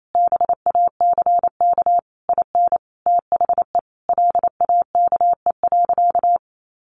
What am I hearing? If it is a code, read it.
Morse code, heard as a keyed tone, about 28 wpm, 700 Hz. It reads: BACK IN THE LAKE.
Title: Sentences from Top 1100 Words 28wpm